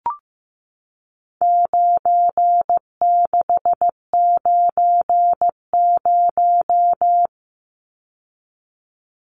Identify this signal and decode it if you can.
Morse code, heard as a keyed tone, about 15 wpm, 700 Hz. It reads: 9690
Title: Numbers - 4-Digit 15wpm